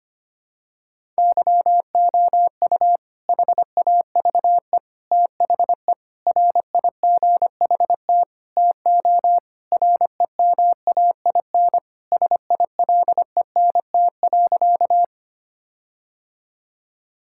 Morse code, heard as a keyed tone, about 25 wpm, 700 Hz. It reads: YOU HAVE THE RIGHT TO REMAIN SILENT.
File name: practice / 25wpm / Sentences with 4-Word Phrases - 25wpm